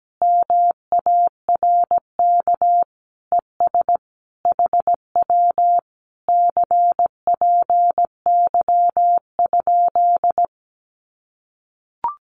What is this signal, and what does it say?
Morse code, heard as a keyed tone, about 17 wpm, 700 Hz. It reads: MARK ES HW CPY?
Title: Common QSO Phrases Encore 17wpm